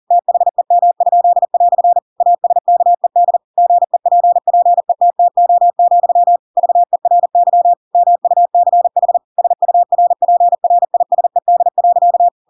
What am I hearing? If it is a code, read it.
Morse code, heard as a keyed tone, about 40 wpm, 700 Hz. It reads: THEM?" ASKED GEPPETTO, VERY MUCH SURPRISED.
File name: Pinocchio / Pinocchio ch7 40 40 ik1hge